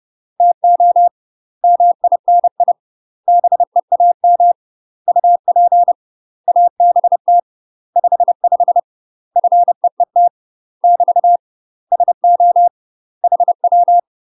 Morse code, heard as a keyed tone, about 30 wpm, 700 Hz. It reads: TO MINI BEAM UP ABT 55 FEET <BT> SO HW
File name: UK MASTER QSOs TRAINING/30WPM